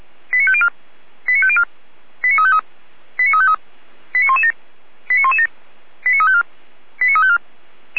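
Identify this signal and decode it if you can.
Five-tone selective call (ZVEI), heard as quick runs of five tones, tones of about 70 ms. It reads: 89504 89504 89353 89353 89117 89117 89354 89354
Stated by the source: ZVEI five-tone call 88133 received on SDR, recorded by feuerwehr-orlamuende.org fire brigade.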